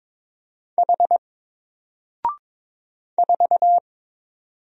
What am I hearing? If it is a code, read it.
Morse code, heard as a keyed tone, about 22 wpm, 700 Hz. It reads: H 4
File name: Single-Letter-Number-22wpm